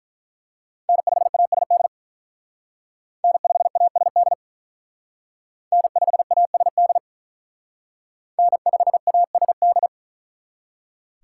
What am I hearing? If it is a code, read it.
Morse code, heard as a keyed tone, about 43 wpm, 700 Hz. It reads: N5ASD N5ASD N5ASD N5ASD
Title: Speed Racing - USA 1x3 Call Signs 35wpm